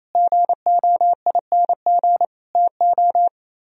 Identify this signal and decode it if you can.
Morse code, heard as a keyed tone, about 35 wpm, 700 Hz. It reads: GOING TO
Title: Speed-Racing - Sets of 3 Words 28wpm